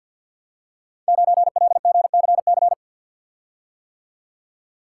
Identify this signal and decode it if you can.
Morse code, heard as a keyed tone, about 47 wpm, 700 Hz. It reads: 9LGCC
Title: Code Groups - 5 Letter:Numbers - 50wpm